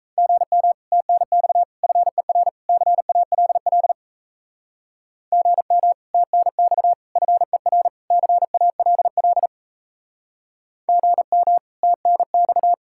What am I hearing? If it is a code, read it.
Morse code, heard as a keyed tone, about 40 wpm, 700 Hz. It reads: GM TNX FER CALL GM TNX FER CALL GM TNX